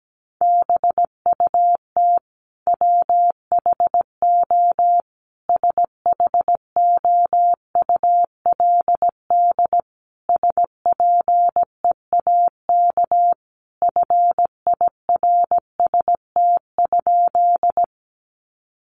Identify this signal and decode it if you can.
Morse code, heard as a keyed tone, about 17 wpm, 700 Hz. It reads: BUT WHO SHOULD SPEAK FIRST?
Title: Sentences from Top 1100 Words 17wpm